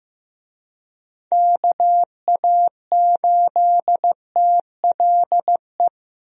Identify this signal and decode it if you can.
Morse code, heard as a keyed tone, about 15 wpm, 700 Hz. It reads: KA8TLE
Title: USA Call Signs - 6 Characters - Repeated 2x - 15wpm